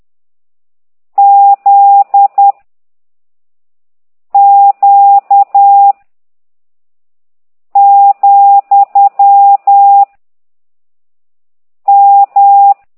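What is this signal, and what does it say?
Morse code, heard as a keyed tone, about 10 wpm, 800 Hz. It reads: Z Q , M
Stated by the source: Morse code lesson 5 exercise 10 by CT2IDW, containing only characters M, O, G, Z, Q and comma.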